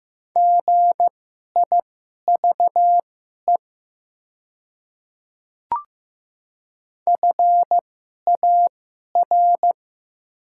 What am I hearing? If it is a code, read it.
Morse code, heard as a keyed tone, about 15 wpm, spaced as for 11 wpm, 700 Hz. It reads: GIVE FAR
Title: 2x Character Spacing - Top 300 Words - 15wpm